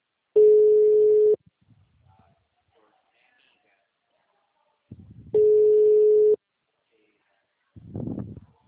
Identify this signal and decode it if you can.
Ringback tone (Europe, 425 Hz), with speech.
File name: out- -2013